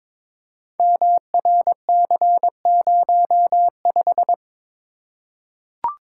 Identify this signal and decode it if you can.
Morse code, heard as a keyed tone, about 23 wpm, 700 Hz. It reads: MRC05
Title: Code Groups - 5 Letter:Numbers - 22wpm